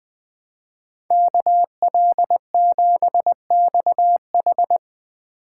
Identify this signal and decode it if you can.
Morse code, heard as a keyed tone, about 24 wpm, 700 Hz. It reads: KL7XH
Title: Speed Racing - USA 2x2 Call Signs 20wpm